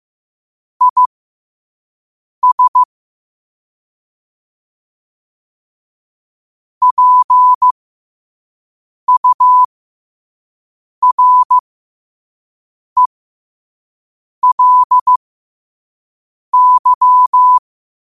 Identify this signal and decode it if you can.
Morse code, heard as a keyed tone, about 15 wpm, spaced as for 5 wpm, 1000 Hz. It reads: IS PURELY
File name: qotdfiles / 5wpm